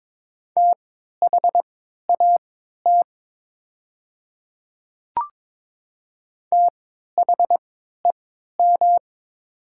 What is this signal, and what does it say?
Morse code, heard as a keyed tone, about 23 wpm, spaced as for 13 wpm, 700 Hz. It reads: THAT THEM
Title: 3x Character Spacing - Top 100 Words - 22wpm